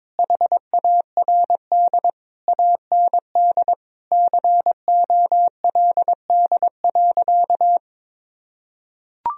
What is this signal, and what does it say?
Morse code, heard as a keyed tone, about 21 wpm, 700 Hz. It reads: HARD AND COLD.
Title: Sentences from Top 1000 Words - Review Part 1 - 22wpm